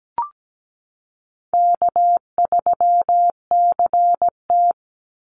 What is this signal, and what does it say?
Morse code, heard as a keyed tone, about 17 wpm, 700 Hz. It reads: K3CT
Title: Easy Call Signs - 17wpm